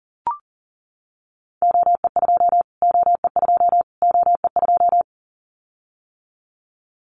Morse code, heard as a keyed tone, about 40 wpm, 700 Hz. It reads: OE2 OE2 OE2
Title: Callsign Prefixes for Top 25 DXCC - Repeated 3x - 40wpm